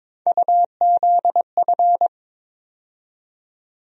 Morse code, heard as a keyed tone, about 22 wpm, 700 Hz. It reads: UZF
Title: USA Call Sign Postfixes - 3 Letters 22wpm